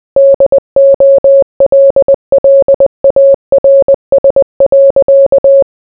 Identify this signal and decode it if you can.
Morse code, heard as a keyed tone, about 20 wpm, 550 Hz. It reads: DOLLARS.